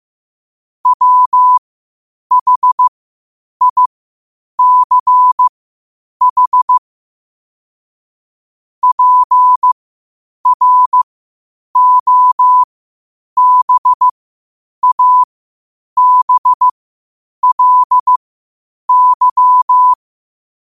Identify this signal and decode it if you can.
Morse code, heard as a keyed tone, about 15 wpm, spaced as for 8 wpm, 1000 Hz. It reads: WHICH PROBABLY
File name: qotdfiles / 8wpm